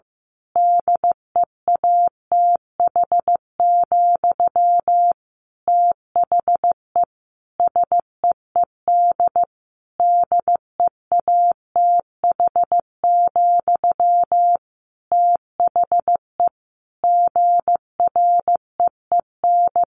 Morse code, heard as a keyed tone, about 15 wpm, 700 Hz. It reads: DEATH, THE SEED DEATH, THE GREEN
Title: A Sound Of Thunder 15wpm